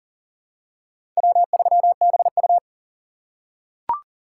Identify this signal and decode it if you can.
Morse code, heard as a keyed tone, about 40 wpm, 700 Hz. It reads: W3BU
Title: USA Call Signs - 4 Characters - 3x Character Spacing - 40wpm